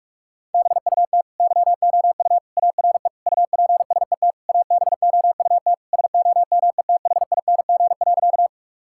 Morse code, heard as a keyed tone, about 46 wpm, 700 Hz. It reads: BUT YOU ARE UPSET ABOUT SOMETHING.
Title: Sentences from Top 1000 Words - Review Part 5 - 45wpm